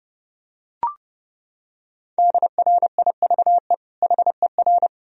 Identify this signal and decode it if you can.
Morse code, heard as a keyed tone, about 30 wpm, 700 Hz. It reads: DRIVE HER
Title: Sentences from Top 1000 Words - Review Part 2 - 30wpm